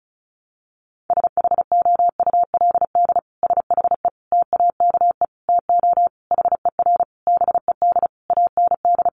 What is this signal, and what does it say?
Morse code, heard as a keyed tone, about 34 wpm, 700 Hz. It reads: SHOULD SHE TAKE TO HER BED AND
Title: Sentences from Top 1200 Words 35wpm